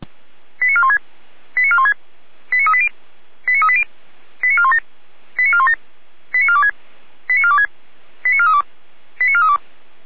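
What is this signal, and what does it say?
Five-tone selective call (ZVEI), heard as quick runs of five tones, tones of about 70 ms. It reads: 89416 89416 89380 89380 89417 89417 89426 89426 89432 89432
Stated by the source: ZVEI five-tone call received by SDR.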